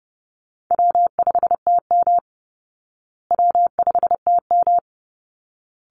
Morse code, heard as a keyed tone, about 30 wpm, 700 Hz. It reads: W5TM W5TM
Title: Easy Call Signs Repeated 3x - 30wpm